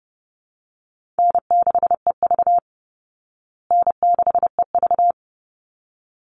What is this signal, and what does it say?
Morse code, heard as a keyed tone, about 30 wpm, 700 Hz. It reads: N6EV N6EV
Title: Call Signs Repeated 3x - 30wpm